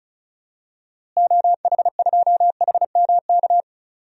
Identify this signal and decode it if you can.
Morse code, heard as a keyed tone, about 35 wpm, 700 Hz. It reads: OH2HMK
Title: DXCC-100 Call Signs - 6 Characters - Repeated 3x - 35wpm